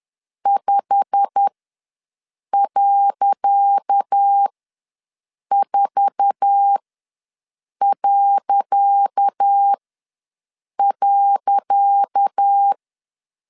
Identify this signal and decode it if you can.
Morse code, heard as a keyed tone, about 10 wpm, spaced as for 6 wpm, 800 Hz. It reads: 5.4..
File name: LEZIONI / 10 SIGNS / 10C DOT ws 10wpm 4213